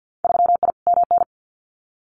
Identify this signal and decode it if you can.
Morse code, heard as a keyed tone, about 50 wpm, 700 Hz. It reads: 3I MN
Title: Winter Field Day Exchanges 50wpm